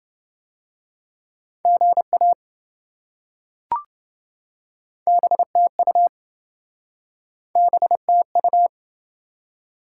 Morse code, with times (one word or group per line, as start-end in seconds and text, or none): 1.6-2.3 GA
5.1-6.1 BTU
7.6-8.7 BTU